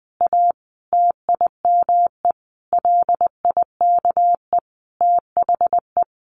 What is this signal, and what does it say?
Morse code, heard as a keyed tone, about 20 wpm, 700 Hz. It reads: A TIME LIKE THE
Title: Sentences from Top 900 Words 20wpm